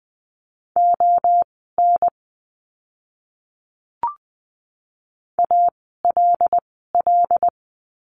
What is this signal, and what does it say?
Morse code, heard as a keyed tone, about 20 wpm, spaced as for 15 wpm, 700 Hz. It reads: ON ALL
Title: 146 Lesson 24f - Letter y - Words 20wpm